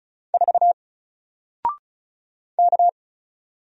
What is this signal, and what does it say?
Morse code, heard as a keyed tone, about 34 wpm, 700 Hz. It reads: 4 K
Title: Single Letter-Number Alt 35wpm